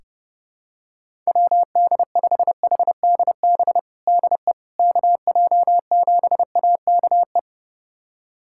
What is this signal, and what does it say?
Morse code, heard as a keyed tone, about 30 wpm, 700 Hz. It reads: WD5HDB DE KJ7AKE
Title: USA Callsign DE Callsign - 6 Characters - 30wpm